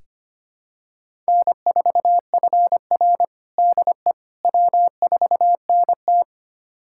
Morse code, heard as a keyed tone, about 25 wpm, 700 Hz. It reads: N4FR DE W4NT
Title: USA Callsign DE Callsign - 4 Characters - 25wpm